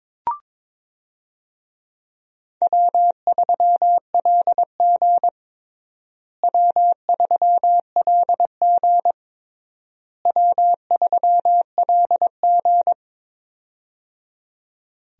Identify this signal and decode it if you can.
Morse code, heard as a keyed tone, about 21 wpm, 700 Hz. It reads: W3LG W3LG W3LG
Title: USA Call Signs - 4 Characters - Repeated 3x - 22wpm